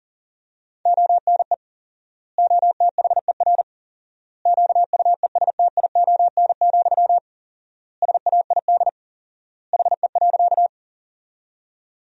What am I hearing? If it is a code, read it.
Morse code, heard as a keyed tone, about 40 wpm, 700 Hz. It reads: ONE OTHER QUESTION, SAID HE.
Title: 4x Word-Spacing - Sentences from Top 400 Words - 40wpm